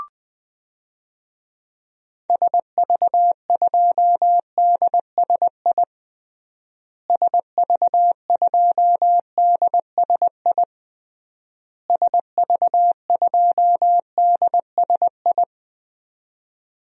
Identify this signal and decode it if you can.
Morse code, heard as a keyed tone, about 19 wpm, 700 Hz. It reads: SV2DSI SV2DSI SV2DSI
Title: DXCC-100 Call Signs - 6 Characters - Repeated 3x - 20wpm